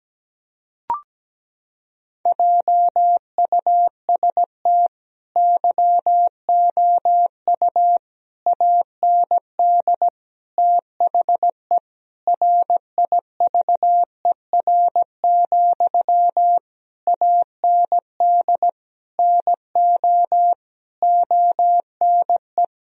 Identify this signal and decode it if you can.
Morse code, heard as a keyed tone, about 17 wpm, 700 Hz. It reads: JUST YOU AND THE RIVER, AND NO ONE
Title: Sentences from Top 1000 Words - Review Part 1 - 17wpm